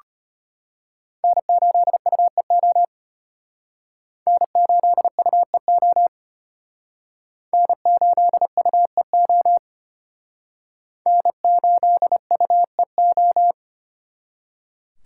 Morse code, heard as a keyed tone, about 32 wpm, 700 Hz. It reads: N8UEO N8UEO N8UEO N8UEO